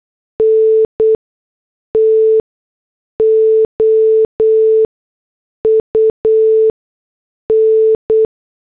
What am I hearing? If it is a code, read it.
Morse code, heard as a keyed tone, about 8 wpm, spaced as for 6 wpm, 440 Hz. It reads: NTO UN